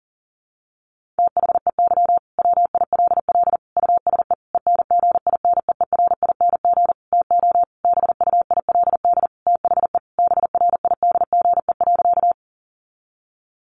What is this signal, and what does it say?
Morse code, heard as a keyed tone, about 40 wpm, 700 Hz. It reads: THEY WILL USE ENGINEERING TO BUILD THE BRIDGE.
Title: Sentences from Top 1500 Words 40wpm